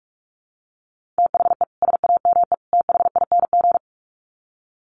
Morse code, heard as a keyed tone, about 44 wpm, 700 Hz. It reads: THE SAME THING
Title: Sets of 3 Words 45wpm